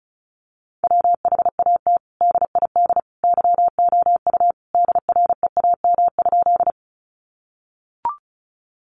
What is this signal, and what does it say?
Morse code, heard as a keyed tone, about 34 wpm, 700 Hz. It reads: WHAT DID YOU DREAM?
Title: Sentences from Top 1500 Words 35wpm